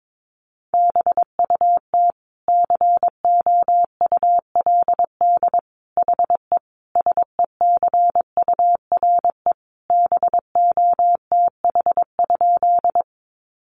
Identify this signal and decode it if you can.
Morse code, heard as a keyed tone, about 22 wpm, 700 Hz. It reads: BUT COULD HE SECURE BOTH?
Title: Sentences from Top 1200 Words 22wpm